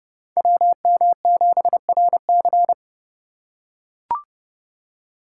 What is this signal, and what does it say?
Morse code, heard as a keyed tone, about 28 wpm, 700 Hz. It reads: WM7RC